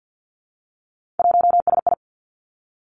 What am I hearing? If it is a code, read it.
Morse code, heard as a keyed tone, about 50 wpm, 700 Hz. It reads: 1SI
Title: Three Challenging Letter-Number Combinations 50wpm